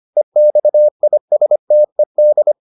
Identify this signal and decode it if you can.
Morse code, heard as a keyed tone, about 25 wpm, 600 Hz. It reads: EXISTED